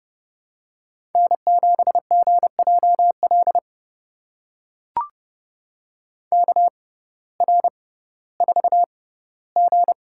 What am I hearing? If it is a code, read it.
Morse code, heard as a keyed tone, about 29 wpm, spaced as for 16 wpm, 700 Hz. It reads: N7GJL KR4G
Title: USA Call Signs - 5 Characters - 6x Character Spacing - 30wpm